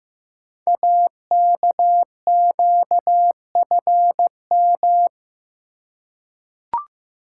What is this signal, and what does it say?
Morse code, heard as a keyed tone, about 15 wpm, 700 Hz. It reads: AKQFM
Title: Code Groups - 5 Letter:Numbers - 15wpm